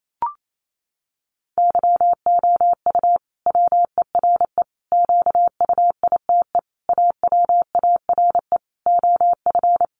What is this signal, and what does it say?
Morse code, heard as a keyed tone, about 28 wpm, 700 Hz. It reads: YOU WERE QUITE AWARE OF
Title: Sentences from Top 1200 Words 28wpm